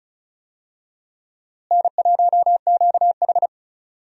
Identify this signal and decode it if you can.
Morse code, heard as a keyed tone, about 36 wpm, 700 Hz. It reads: N1QH
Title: USA Call Signs - 4 Characters - Repeated 3x - 35wpm